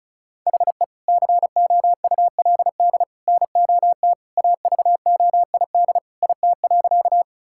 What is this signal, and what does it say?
Morse code, heard as a keyed tone, about 36 wpm, 700 Hz. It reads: HE COULD NOT AVOID IT.
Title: Sentences from Top 1000 Words - Review Part 6 - 35wpm